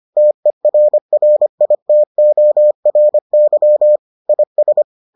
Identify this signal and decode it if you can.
Morse code, heard as a keyed tone, about 25 wpm, 600 Hz. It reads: TERRITORY IS